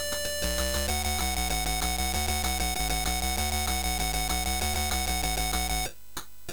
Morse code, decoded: none